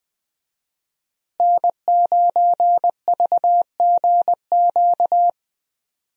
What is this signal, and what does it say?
Morse code, heard as a keyed tone, about 21 wpm, 700 Hz. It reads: N9VGQ
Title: USA Call Signs - 5 Characters - Repeated 2x - 20wpm